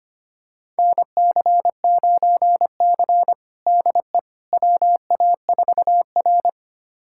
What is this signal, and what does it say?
Morse code, heard as a keyed tone, about 25 wpm, 700 Hz. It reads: NC9C DE WA4R
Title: USA Callsign DE Callsign - 4 Characters - 25wpm